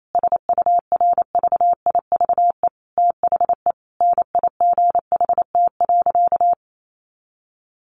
Morse code, heard as a keyed tone, about 28 wpm, 700 Hz. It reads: SURVIVE THE NIGHT.
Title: Sentences from Top 1300 Words 28wpm